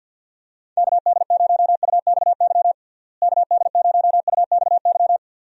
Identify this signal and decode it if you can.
Morse code, heard as a keyed tone, about 51 wpm, 700 Hz. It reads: KD0UXY KD0UXY
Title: USA Call Signs - 6 Characters - Repeated 3x - 50wpm